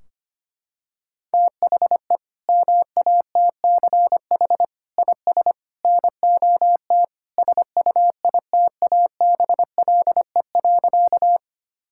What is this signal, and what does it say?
Morse code, heard as a keyed tone, about 25 wpm, 700 Hz. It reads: THE MATCH IS NOT SUITABLE.
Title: Sentences from Top 1600 Words 25wpm